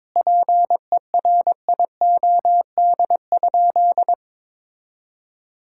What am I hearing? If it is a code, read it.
Morse code, heard as a keyed tone, about 21 wpm, 700 Hz. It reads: PERIOD?
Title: Sentences from Top 1000 Words - Review Part 5 - 22wpm